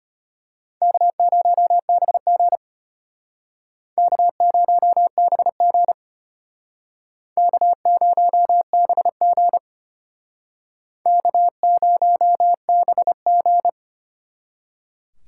Morse code, heard as a keyed tone, about 35 wpm, 700 Hz. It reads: K0BG K0BG K0BG K0BG